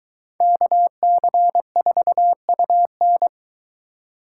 Morse code, heard as a keyed tone, about 22 wpm, 700 Hz. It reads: KC4UN